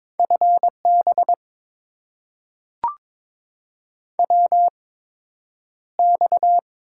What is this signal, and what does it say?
Morse code, heard as a keyed tone, about 22 wpm, spaced as for 9 wpm, 700 Hz. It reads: FB WX
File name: Top 90 QSO Elements - 8x Character Spacing - 22wpm